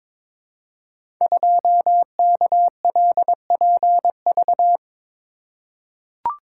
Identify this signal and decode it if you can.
Morse code, heard as a keyed tone, about 21 wpm, 700 Hz. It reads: 2KLPV
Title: Code Groups - 5 Letter:Numbers - 22wpm